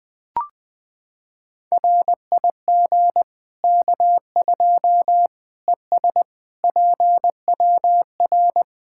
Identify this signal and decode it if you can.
Morse code, heard as a keyed tone, about 20 wpm, 700 Hz. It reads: RIG K2 ES PWR
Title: Practice QSO - Exchange 2 - Info - 20wpm